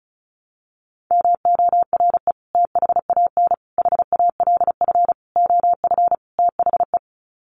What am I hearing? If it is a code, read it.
Morse code, heard as a keyed tone, about 34 wpm, 700 Hz. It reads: MORE THAN HALF OF THE
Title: Sets of 5 Words 35wpm